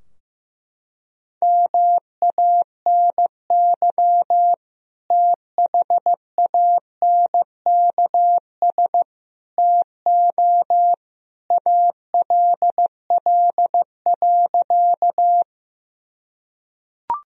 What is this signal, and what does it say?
Morse code, heard as a keyed tone, about 15 wpm, 700 Hz. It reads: MANY THANKS TO ALL.
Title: Sentences from Top 1000 Words - Review Part 3 - 15wpm